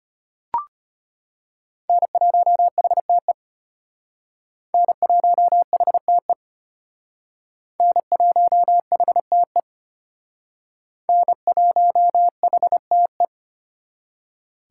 Morse code, heard as a keyed tone, about 32 wpm, 700 Hz. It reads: N1HTE N1HTE N1HTE N1HTE